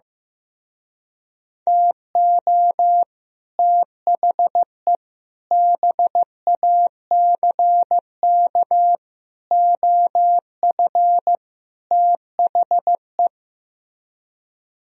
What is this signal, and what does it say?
Morse code, heard as a keyed tone, about 15 wpm, 700 Hz. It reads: TO THE BACK OF THE
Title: Speed-Racing - Sets of 5 Words 15wpm